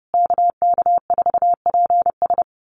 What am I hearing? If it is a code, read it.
Morse code, heard as a keyed tone, about 30 wpm, 700 Hz. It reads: KK4PS
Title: USA Call Signs Repeated 3x 30wpm